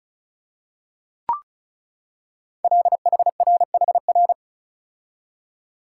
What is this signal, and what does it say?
Morse code, heard as a keyed tone, about 36 wpm, 700 Hz. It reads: LHRHR
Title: Code Groups - 5 Letter:Numbers - 35wpm